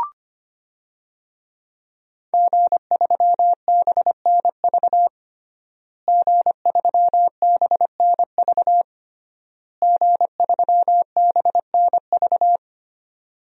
Morse code, heard as a keyed tone, about 25 wpm, 700 Hz. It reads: G3BNV G3BNV G3BNV
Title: DXCC-100 Call Signs - 5 Characters - Repeated 3x - 25wpm